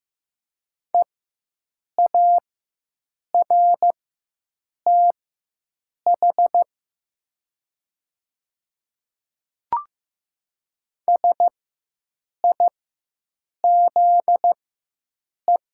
Morse code, heard as a keyed tone, about 15 wpm, spaced as for 7 wpm, 700 Hz. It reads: EARTH SIZE
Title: 4x Character Spacing - Top 500 Words - 15wpm